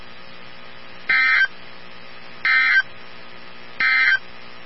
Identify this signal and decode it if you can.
Emergency Alert System (SAME) tones, heard as harsh buzzing bursts. It is heard from 1.0 to 4.3 s.
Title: NOAA Weather Radio 162.550